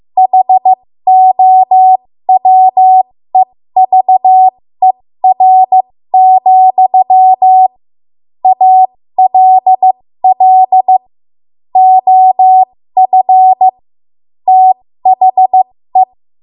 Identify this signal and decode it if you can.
Morse code, heard as a keyed tone, about 15 wpm, spaced as for 13 wpm, 750 Hz. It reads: HOWEVER, ALL OF THE
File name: Morse / 13WPM